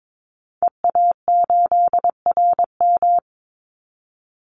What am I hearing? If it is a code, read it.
Morse code, heard as a keyed tone, about 22 wpm, 700 Hz. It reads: EA8RM
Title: Call Signs - 22wpm